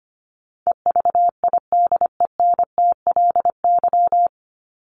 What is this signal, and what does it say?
Morse code, heard as a keyed tone, about 25 wpm, 700 Hz. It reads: EVIDENTLY
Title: LY Suffix - Top 100 Words with Base Words 25wpm